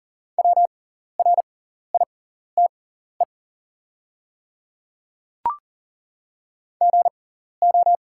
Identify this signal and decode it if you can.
Morse code, heard as a keyed tone, about 40 wpm, spaced as for 14 wpm, 700 Hz. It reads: WRITE GO
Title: 6x Character Spacing - Top 100 Words - 40wpm